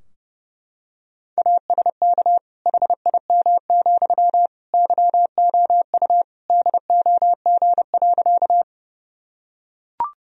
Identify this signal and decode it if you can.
Morse code, heard as a keyed tone, about 32 wpm, 700 Hz. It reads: ASK HIM, YOU DOG.
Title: Sentences from Top 1000 Words - Review Part 2 - 30wpm